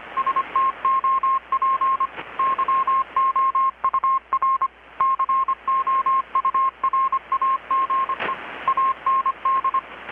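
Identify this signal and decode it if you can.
Morse code, heard as a keyed tone, about 25 wpm, 1070 Hz, with radio noise. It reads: STOP YOUR COURAGE AND
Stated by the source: Morse from KPH's Night of Nights XX maritime radiotelegraphy event, received in Chiapas, Mexico about 3,800 km away on a Kenwood R-600 with a grounded 10 m long-wire antenna.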